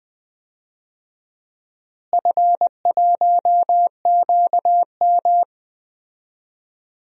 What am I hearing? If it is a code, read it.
Morse code, heard as a keyed tone, about 20 wpm, 700 Hz. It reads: F1QM
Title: DXCC-100 Call Signs - 4 Characters - Repeated 2x - 20wpm